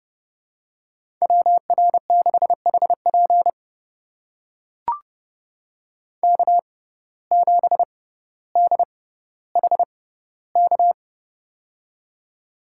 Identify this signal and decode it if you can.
Morse code, heard as a keyed tone, about 30 wpm, spaced as for 15 wpm, 700 Hz. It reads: WR6HP K7DHK
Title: USA Call Signs - 5 Characters - 6x Character Spacing - 30wpm